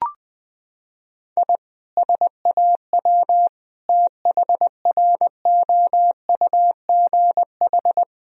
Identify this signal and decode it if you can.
Morse code, heard as a keyed tone, about 20 wpm, 700 Hz. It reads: I SAW THROUGH
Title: Sentences from Top 1000 Words - Review Part 1 - 20wpm